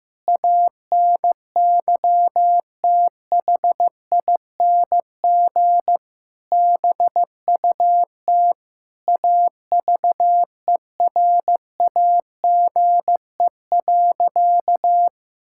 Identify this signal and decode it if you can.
Morse code, heard as a keyed tone, about 15 wpm, 700 Hz. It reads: ANYTHING BUT AVERAGE.
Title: Sentences from Top 1000 Words - Review Part 6 - 15wpm